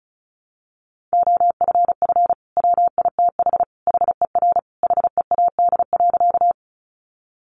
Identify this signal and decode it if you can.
Morse code, heard as a keyed tone, about 34 wpm, 700 Hz. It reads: OFF WITH HER HEAD.
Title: Sentences from Top 300 Words 35wpm